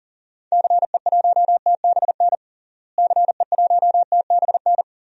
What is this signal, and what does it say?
Morse code, heard as a keyed tone, about 40 wpm, 700 Hz. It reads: CE1TBN CE1TBN
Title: DXCC-100 Call Signs - 6 Characters - Repeated 3x - 40wpm